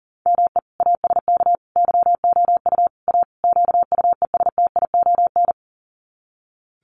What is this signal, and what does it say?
Morse code, heard as a keyed tone, about 40 wpm, 700 Hz. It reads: ME ASK YOU A QUESTION
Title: Sets of 5 Words Encore2 40wpm